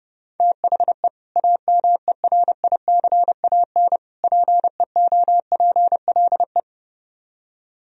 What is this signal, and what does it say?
Morse code, heard as a keyed tone, about 28 wpm, 700 Hz. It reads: THE AMERICAN PEOPLE
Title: Speed-Racing - Sets of 3 Words 30wpm